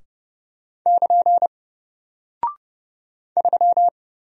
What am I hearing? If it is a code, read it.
Morse code, heard as a keyed tone, about 30 wpm, 700 Hz. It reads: <KN> 3
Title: Single Character (Letter:Number:Punctuation:Prosign) Alt 30wpm